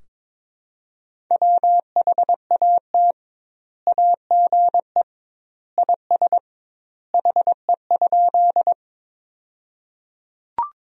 Morse code, heard as a keyed tone, about 22 wpm, 700 Hz. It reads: WHAT AGE IS HE?
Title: 2x Word-Spacing - Sentences from Top 500 Words - 22wpm